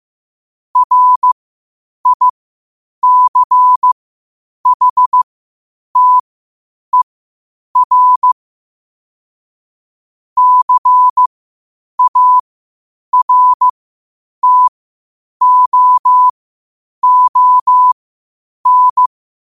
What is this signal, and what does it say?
Morse code, heard as a keyed tone, about 15 wpm, spaced as for 8 wpm, 1000 Hz. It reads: RICHTER CARTOON
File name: qotdfiles / 8wpm